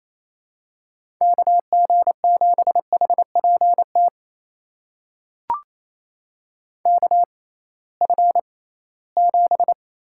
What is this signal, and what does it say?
Morse code, heard as a keyed tone, about 28 wpm, spaced as for 18 wpm, 700 Hz. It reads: KG7HPT KF7
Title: USA Call Signs - 6 Characters - 6x Character Spacing - 28wpm